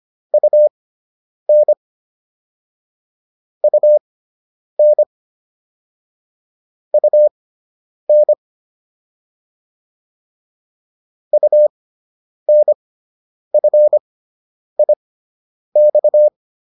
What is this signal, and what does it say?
Morse code, heard as a keyed tone, about 25 wpm, spaced as for 9 wpm, 600 Hz. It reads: UN UN UN UNFIX